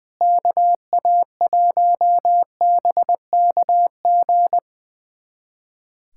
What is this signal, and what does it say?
Morse code, heard as a keyed tone, about 20 wpm, 700 Hz. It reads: KA1BKG
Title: USA Callsign DE Callsign - 6 Characters - 20wpm